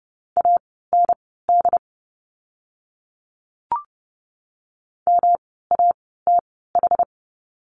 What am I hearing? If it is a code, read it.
Morse code, heard as a keyed tone, about 30 wpm, spaced as for 17 wpm, 700 Hz. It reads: AND MATH
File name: course / 30:10wpm / 108 Lesson 18f - Letter m - Words 30wpm